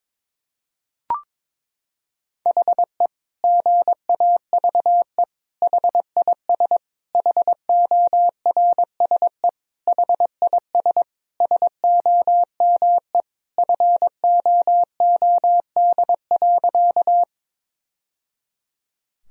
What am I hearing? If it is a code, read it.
Morse code, heard as a keyed tone, about 22 wpm, 700 Hz. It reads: HE GAVE HIS HORSE HIS SOME FOOD.
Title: Sentences from Top 1000 Words - Review Part 1 - 22wpm